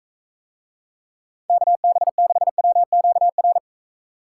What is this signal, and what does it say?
Morse code, heard as a keyed tone, about 44 wpm, 700 Hz. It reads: KB6WQR